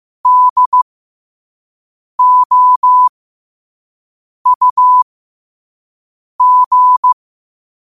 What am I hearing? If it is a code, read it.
Morse code, heard as a keyed tone, about 15 wpm, spaced as for 5 wpm, 1000 Hz. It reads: DOUG